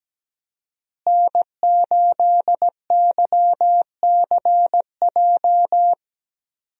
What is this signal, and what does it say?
Morse code, heard as a keyed tone, about 17 wpm, 700 Hz. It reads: N8YCJ
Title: USA Call Signs - 5 Characters - Repeated 3x - 17wpm